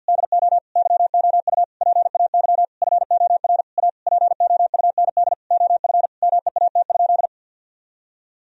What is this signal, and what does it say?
Morse code, heard as a keyed tone, about 47 wpm, 700 Hz. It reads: DO YOU PAY FOR A POUND OF MEAT?
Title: Sentences from Top 1000 Words - Review Part 3 - 50wpm